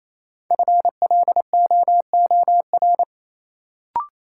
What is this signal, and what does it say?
Morse code, heard as a keyed tone, about 28 wpm, 700 Hz. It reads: FLOOR
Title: Sets of 3 Words - New Word Drill - 28wpm